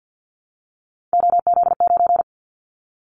Morse code, heard as a keyed tone, about 49 wpm, 700 Hz. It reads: OZ9
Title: Callsign Prefixes for Top 100 DXCC - 50wpm